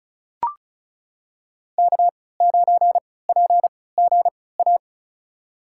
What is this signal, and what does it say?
Morse code, heard as a keyed tone, about 34 wpm, spaced as for 20 wpm, 700 Hz. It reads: K9PGA
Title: USA Call Signs - 3x Character Spacing - 35wpm